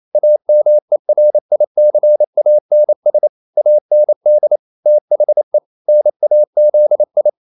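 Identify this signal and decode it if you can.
Morse code, heard as a keyed tone, about 28 wpm, 600 Hz. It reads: AMERICANS AND THE NAZI
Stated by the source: Morse code practice at 28 wpm, text of Ken Follett's novel Never in English, from CW med Gnister.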